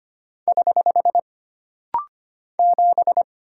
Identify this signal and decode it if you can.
Morse code, heard as a keyed tone, about 25 wpm, 700 Hz. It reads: <HH> 7
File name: Single Character (Letter:Number:Punctuation:Prosign) - Rapid-Fire - 25wpm